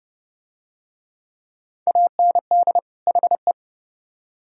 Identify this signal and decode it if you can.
Morse code, heard as a keyed tone, about 28 wpm, 700 Hz. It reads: AND HE